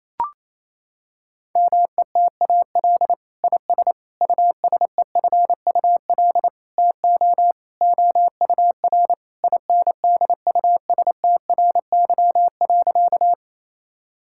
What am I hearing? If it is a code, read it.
Morse code, heard as a keyed tone, about 28 wpm, 700 Hz. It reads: METAL IS USEFUL TO OUR INDUSTRY.
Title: Sentences from Top 1000 Words - Review Part 6 - 28wpm